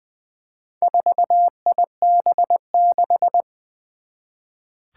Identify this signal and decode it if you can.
Morse code, heard as a keyed tone, about 20 wpm, 700 Hz. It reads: 4IB6
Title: Four Challenging Letter-Number Combinations 20wpm